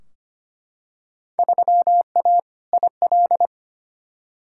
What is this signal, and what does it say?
Morse code, heard as a keyed tone, about 25 wpm, 700 Hz. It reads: 3A IL
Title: Field Day Exchanges - 25wpm